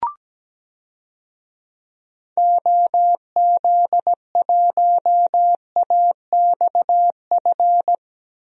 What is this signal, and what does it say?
Morse code, heard as a keyed tone, about 17 wpm, 700 Hz. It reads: OZ1AXF
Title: DXCC-100 Call Signs - 6 Characters - 17wpm